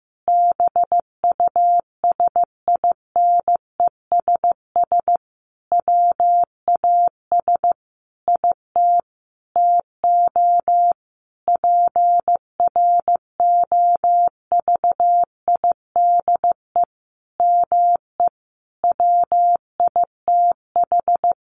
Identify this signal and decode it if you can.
Morse code, heard as a keyed tone, about 15 wpm, 700 Hz. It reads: BUSINESS WAS IT TO PROVIDE ME WITH
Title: Sentences from Top 1500 Words 15wpm